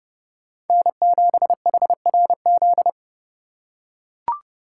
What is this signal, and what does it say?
Morse code, heard as a keyed tone, about 30 wpm, 700 Hz. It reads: N7HRZ
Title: USA Call Signs - 2x Character Spacing - 30wpm